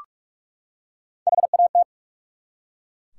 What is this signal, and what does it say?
Morse code, heard as a keyed tone, about 44 wpm, 700 Hz. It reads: HAT
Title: Three Letter Words - 45wpm